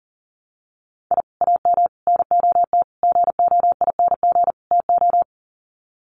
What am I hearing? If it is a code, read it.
Morse code, heard as a keyed tone, about 40 wpm, 700 Hz. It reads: I AM NOT GOING TO
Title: Sets of 5 Words Encore 40wpm